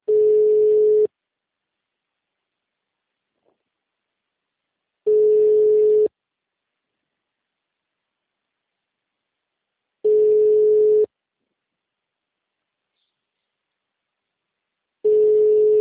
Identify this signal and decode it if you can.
Ringback tone (Europe, 425 Hz).
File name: out- -2044- - - .28